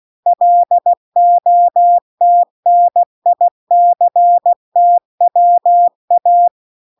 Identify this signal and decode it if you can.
Morse code, heard as a keyed tone, about 16 wpm, 700 Hz. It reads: LOTNICTWA